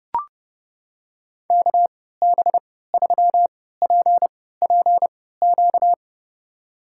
Morse code, heard as a keyed tone, about 32 wpm, spaced as for 17 wpm, 700 Hz. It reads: KB3PPQ
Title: USA Call Signs - 3x Character Spacing - 30wpm